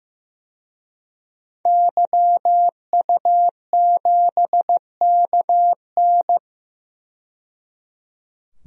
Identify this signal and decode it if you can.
Morse code, heard as a keyed tone, about 15 wpm, 700 Hz. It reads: YU7KN
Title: DXCC-100 Call Signs - 5 Characters - 15wpm